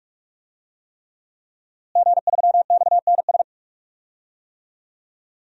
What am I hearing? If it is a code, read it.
Morse code, heard as a keyed tone, about 42 wpm, 700 Hz. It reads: G3XNS
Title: DXCC-100 Call Signs - 5 Characters - Repeated 3x - 45wpm